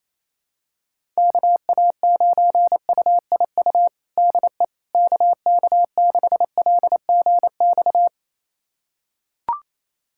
Morse code, heard as a keyed tone, about 28 wpm, 700 Hz. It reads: KA9UIU DE KK6LGX
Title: USA Callsign DE Callsign - 6 Characters - 28wpm